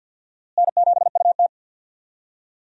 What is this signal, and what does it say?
Morse code, heard as a keyed tone, about 53 wpm, 700 Hz. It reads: N7UT
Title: USA Call Signs - 4 Characters - Repeated 2x - 50wpm